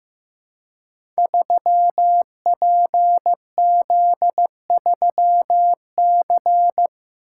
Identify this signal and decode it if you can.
Morse code, heard as a keyed tone, about 15 wpm, 700 Hz. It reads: 3PZ3C
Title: Code Groups - 5 Letter:Numbers - 15wpm